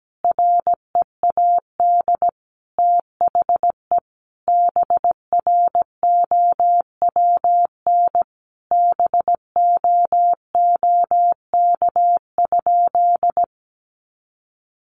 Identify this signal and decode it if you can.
Morse code, heard as a keyed tone, about 17 wpm, 700 Hz. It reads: READ THE BROWN BOOK?
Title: Sentences from Top 1200 Words 17wpm